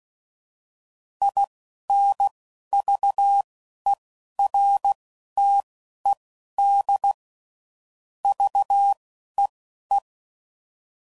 Morse code, heard as a keyed tone, about 16 wpm, spaced as for 11 wpm, 800 Hz. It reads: INVERTED VEE
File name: CW Code Groups @10WPM / fromSM's / Antennas